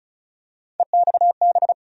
Morse code, heard as a keyed tone, about 36 wpm, 700 Hz. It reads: EXB
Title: USA Call Sign Postfixes - 3 Letters 35wpm